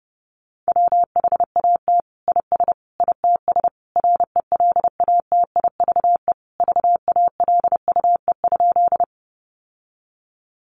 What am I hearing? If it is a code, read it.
Morse code, heard as a keyed tone, about 30 wpm, 700 Hz. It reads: WHAT IS ITS RELATIVE VALUE?
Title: Sentences from Top 1400 Words 30wpm